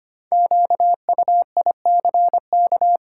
Morse code, heard as a keyed tone, about 25 wpm, 700 Hz. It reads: QUICK
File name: 5 Letter Words - First 3 Letters Given - 25wpm